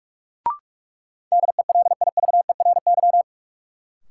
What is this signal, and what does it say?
Morse code, heard as a keyed tone, about 46 wpm, 700 Hz. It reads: DELIVERY
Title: Sentences from Top 1500 Words - New Word Drill Alt - 45wpm